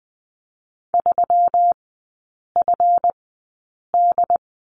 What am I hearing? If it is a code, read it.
Morse code, heard as a keyed tone, about 20 wpm, spaced as for 8 wpm, 700 Hz. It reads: 3FD
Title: 166 Lesson 27e - Number 7 - Three Characters 20wpm